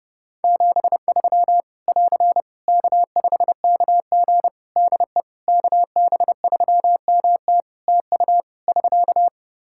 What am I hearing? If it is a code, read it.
Morse code, heard as a keyed tone, about 32 wpm, 700 Hz. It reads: 73 <AR> K5KG DE KB3MT TU <SK>
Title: K1USN SST QSO Practice 30wpm